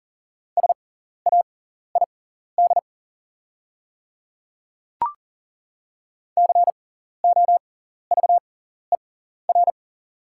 Morse code, heard as a keyed tone, about 41 wpm, spaced as for 14 wpm, 690 Hz. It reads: SAID COVER
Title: 6x Character Spacing - Top 300 Words - 40wpm